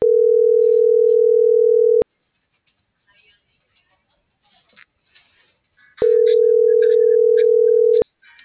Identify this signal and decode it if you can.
Ringback tone (North America), 440 + 480 Hz.